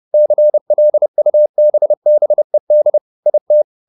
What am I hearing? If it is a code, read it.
Morse code, heard as a keyed tone, about 30 wpm, 600 Hz. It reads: CLUBBED IT